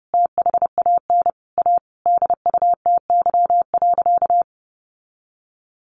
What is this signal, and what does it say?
Morse code, heard as a keyed tone, about 30 wpm, 700 Hz. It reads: THAN A DUTY.
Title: Sentences from Top 1500 Words 30wpm